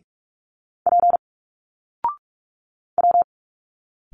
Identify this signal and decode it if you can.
Morse code, heard as a keyed tone, about 46 wpm, 700 Hz. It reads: P W
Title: Single Letters 45wpm